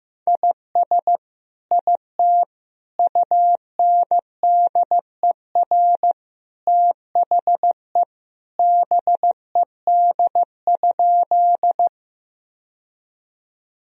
Morse code, heard as a keyed tone, about 15 wpm, 700 Hz. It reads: IS IT UNDER THE BED?
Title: Sentences from Top 1000 Words - Review Part 1 - 15wpm